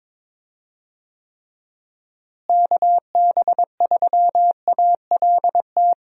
Morse code, heard as a keyed tone, about 22 wpm, 700 Hz. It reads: KB3ALT
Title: USA Call Signs - 6 Characters - Repeated 3x - 22wpm